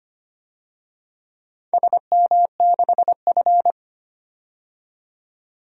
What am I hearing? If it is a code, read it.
Morse code, heard as a keyed tone, about 25 wpm, 700 Hz. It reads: SM6F
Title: DXCC-100 Call Signs - 4 Characters - 25wpm